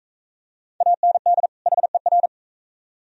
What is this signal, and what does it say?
Morse code, heard as a keyed tone, about 38 wpm, 690 Hz. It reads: AND HER